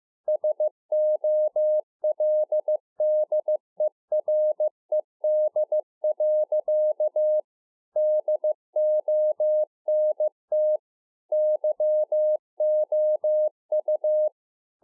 Morse code, heard as a keyed tone, about 15 wpm, 610 Hz. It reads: SOLDERED. DONT YOU
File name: 2011-04-mp3 / A SMALL TRANCEIVER 2 15WPM